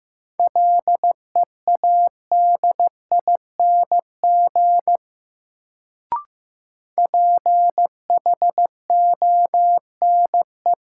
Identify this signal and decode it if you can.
Morse code, heard as a keyed tone, about 15 wpm, 700 Hz. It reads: LEADING PHONE
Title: Sentences from Top 1100 Words - New Word Drill - 15wpm